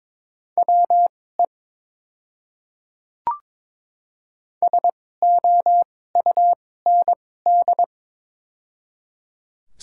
Morse code, heard as a keyed tone, about 22 wpm, spaced as for 16 wpm, 700 Hz. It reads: WE SOUND
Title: 2x Character Spacing - Top 500 Words - 22wpm